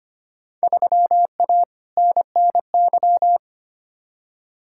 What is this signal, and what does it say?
Morse code, heard as a keyed tone, about 25 wpm, 700 Hz. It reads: 3A NNY
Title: Field Day Exchanges - 25wpm